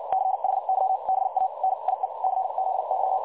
Morse code, heard as a keyed tone, about 56 wpm, spaced as for 45 wpm, 760 Hz, with noise. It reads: CHYBA NIE CZY
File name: dlazaawansowanych / nagraniazpasma / websdr recording 3565.0kHz